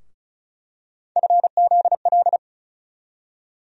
Morse code, heard as a keyed tone, about 36 wpm, 700 Hz. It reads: FZL
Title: USA Call Sign Postfixes - 3 Letters 35wpm